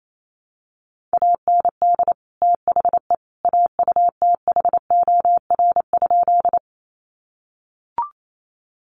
Morse code, heard as a keyed tone, about 28 wpm, 700 Hz. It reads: AND THE AUTHOR?
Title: Sentences from Top 1200 Words 28wpm